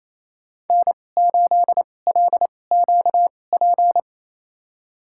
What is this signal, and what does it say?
Morse code, heard as a keyed tone, about 28 wpm, spaced as for 20 wpm, 700 Hz. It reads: N8LQP
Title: USA Call Signs - 5 Characters - 2x Character Spacing - 28wpm